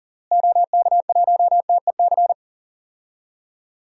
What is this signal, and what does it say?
Morse code, heard as a keyed tone, about 40 wpm, 700 Hz. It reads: OK1TEC
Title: DXCC-100 Call Signs - 6 Characters - Repeated 3x - 40wpm